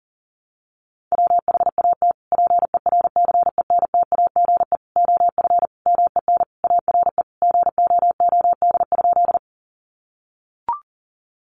Morse code, heard as a keyed tone, about 40 wpm, 700 Hz. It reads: WHAT PERCENTAGE OF MEN ARE GOOD?
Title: Sentences from Top 1300 Words 40wpm